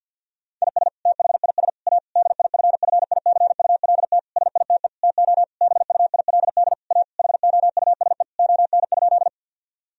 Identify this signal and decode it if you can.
Morse code, heard as a keyed tone, about 50 wpm, 700 Hz. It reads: IS THIS A DIFFICULT SITE TO BUILD A HOUSE ON?
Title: Sentences from Top 1000 Words - Review Part 4 - 50wpm